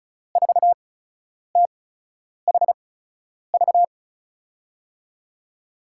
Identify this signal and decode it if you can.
Morse code, heard as a keyed tone, about 36 wpm, spaced as for 10 wpm, 700 Hz. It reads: 4THV
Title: USA Call Signs - 6 Characters - 8x Character Spacing - 35wpm